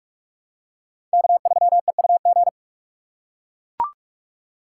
Morse code, heard as a keyed tone, about 44 wpm, 700 Hz. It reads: K3EUG
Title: USA Call Signs - 5 Characters - 2x Character Spacing - 45wpm